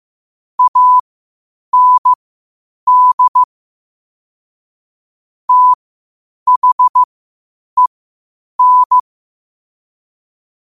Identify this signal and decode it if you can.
Morse code, heard as a keyed tone, about 15 wpm, spaced as for 8 wpm, 1000 Hz. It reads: AND THEN
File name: qotdfiles / 8wpm